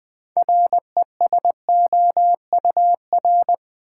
Morse code, heard as a keyed tone, about 20 wpm, 700 Hz. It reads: RESOUR
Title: Build A Word - 2 Letters At A Time - 8-Letter Words - 20wpm